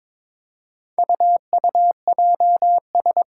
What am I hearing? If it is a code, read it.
Morse code, heard as a keyed tone, about 22 wpm, 700 Hz. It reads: UUJS
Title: Four Challenging Letter-Number Combinations 22wpm